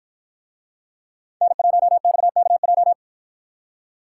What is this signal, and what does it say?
Morse code, heard as a keyed tone, about 50 wpm, 700 Hz. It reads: N1XKJ